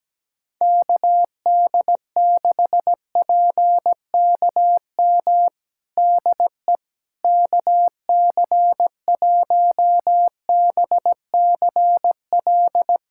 Morse code, heard as a keyed tone, about 17 wpm, 700 Hz. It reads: KD6PKM DE KC1BCL
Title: USA Callsign DE Callsign 17wpm